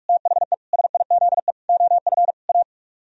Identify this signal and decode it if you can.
Morse code, heard as a keyed tone, about 50 wpm, 700 Hz. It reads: THE SIZE OF A